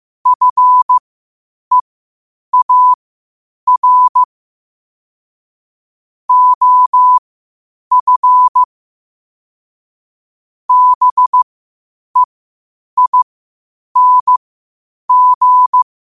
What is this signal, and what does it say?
Morse code, heard as a keyed tone, about 15 wpm, spaced as for 8 wpm, 1000 Hz. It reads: FEAR OF BEING
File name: qotdfiles / 8wpm